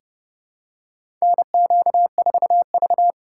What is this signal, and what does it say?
Morse code, heard as a keyed tone, about 30 wpm, 700 Hz. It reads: NQ4V
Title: USA Call Signs - 4 Characters - 4x Character Spacing - 30wpm